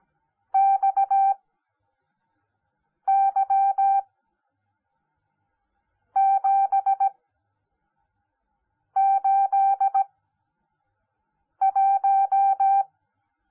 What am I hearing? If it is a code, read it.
Morse code, heard as a keyed tone, about 17 wpm, spaced as for 4 wpm, 780 Hz. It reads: XY781
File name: Onlinelehrgang / morsen / DJ4UF Morsen10